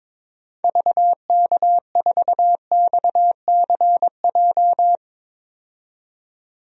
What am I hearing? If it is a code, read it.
Morse code, heard as a keyed tone, about 22 wpm, 700 Hz. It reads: VK4XCJ
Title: DXCC-100 Call Signs - 6 Characters - 22wpm